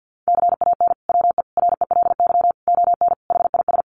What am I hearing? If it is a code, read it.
Morse code, heard as a keyed tone, about 50 wpm, 700 Hz. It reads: CAN WE RELY ON HIS
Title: Sentences from Top 1200 Words 50wpm